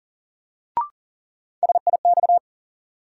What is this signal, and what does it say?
Morse code, heard as a keyed tone, about 40 wpm, 700 Hz. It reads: SIX
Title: Sentences from Top 600 Words - New Word Drill - 40wpm